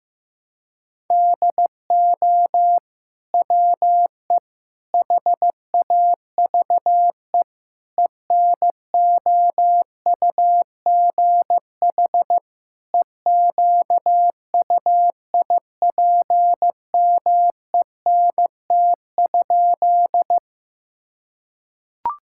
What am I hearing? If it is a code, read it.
Morse code, heard as a keyed tone, about 15 wpm, 700 Hz. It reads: DO WE HAVE ENOUGH EQUIPMENT?
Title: Sentences from Top 1000 Words - Review Part 5 - 15wpm